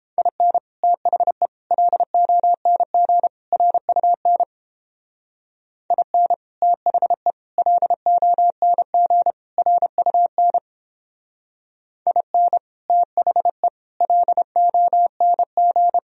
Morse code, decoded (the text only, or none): IN THE LONG RUN IN THE LONG RUN IN THE LONG